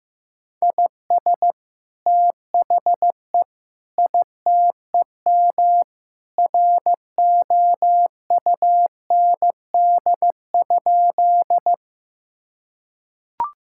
Morse code, heard as a keyed tone, about 15 wpm, 700 Hz. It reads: IS THE ITEM ROUND?
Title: Sentences from Top 1000 Words - Review Part 6 - 15wpm